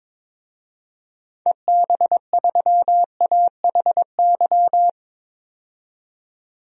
Morse code, heard as a keyed tone, about 21 wpm, 700 Hz. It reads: EB3AHY
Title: DXCC-100 Call Signs - 6 Characters - 22wpm